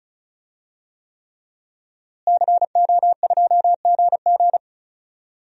Morse code, heard as a keyed tone, about 36 wpm, 700 Hz. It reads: CO2GG
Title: DXCC-100 Call Signs - 5 Characters - Repeated 3x - 35wpm